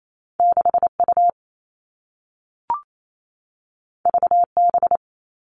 Morse code, heard as a keyed tone, about 28 wpm, 700 Hz. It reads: 6U VB
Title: All Cut Numbers - 2-Digit 28wpm